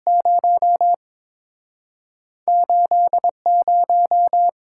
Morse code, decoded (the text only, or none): 0 8 0